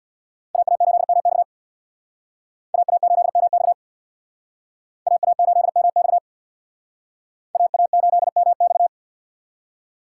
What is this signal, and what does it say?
Morse code, heard as a keyed tone, about 61 wpm, 700 Hz. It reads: AA8MX AA8MX AA8MX AA8MX
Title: Speed Racing - USA 2x2 Call Signs 50wpm